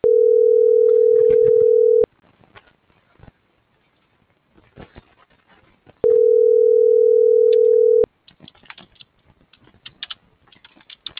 Ringback tone (North America), 440 + 480 Hz.